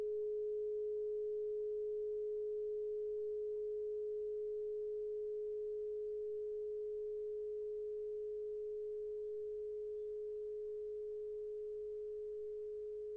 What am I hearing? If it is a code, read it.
Dial tone (Europe, 425 Hz).